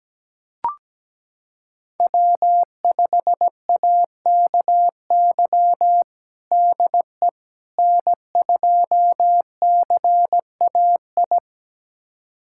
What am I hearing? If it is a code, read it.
Morse code, heard as a keyed tone, about 17 wpm, 700 Hz. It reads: W5AKY DE N2CAI
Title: USA Callsign DE Callsign - 5 Characters - 17wpm